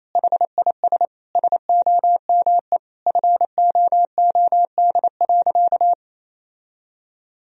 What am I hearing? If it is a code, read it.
Morse code, heard as a keyed tone, about 28 wpm, 700 Hz. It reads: HIS SOME FOOD.
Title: Sentences from Top 1000 Words - Review Part 1 - 28wpm